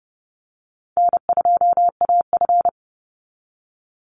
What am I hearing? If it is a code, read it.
Morse code, heard as a keyed tone, about 30 wpm, 700 Hz. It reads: N2AF
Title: NA SOTA Chaser Callsigns 30wpm